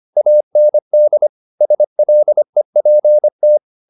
Morse code, heard as a keyed tone, about 25 wpm, 600 Hz. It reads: AND SLEPT
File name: the old man and the sea part 1 25wpm